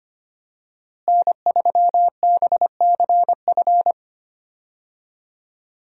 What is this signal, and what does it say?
Morse code, heard as a keyed tone, about 25 wpm, 700 Hz. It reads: N3BCF
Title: USA Call Signs - 5 Characters - Repeated 3x - 25wpm